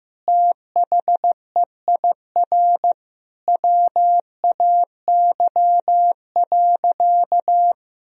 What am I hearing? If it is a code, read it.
Morse code, heard as a keyed tone, about 15 wpm, 700 Hz. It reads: THEIR WAY.
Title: Sentences from Top 1000 Words - Review Part 4 - 15wpm